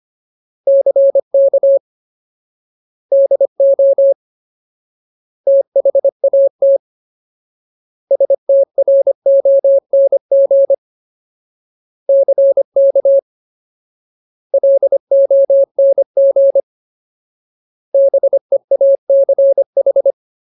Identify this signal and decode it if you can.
Morse code, heard as a keyed tone, about 25 wpm, 550 Hz. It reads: CK DO THAT STRONG CK LONG BEACH
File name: Ultimate Word List / 25WPM Ultimate